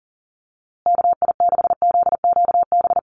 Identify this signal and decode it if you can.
Morse code, heard as a keyed tone, about 40 wpm, 700 Hz. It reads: KI6ZQB
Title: USA Call Signs Repeated 2x 40wpm